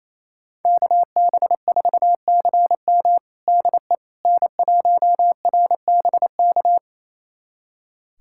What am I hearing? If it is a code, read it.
Morse code, heard as a keyed tone, about 28 wpm, 700 Hz. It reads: KB4CM DE N1RBK
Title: USA Callsign DE Callsign - 5 Characters - 28wpm